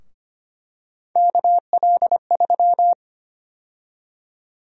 Morse code, heard as a keyed tone, about 25 wpm, 700 Hz. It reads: KL3
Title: USA Call Sign Prefixes - 25wpm